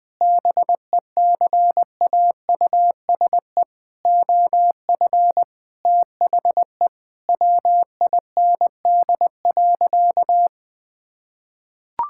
Morse code, heard as a keyed tone, about 20 wpm, 700 Hz. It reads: BECAUSE OF THE WIND.
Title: Sentences with 3-Word Phrases - 20wpm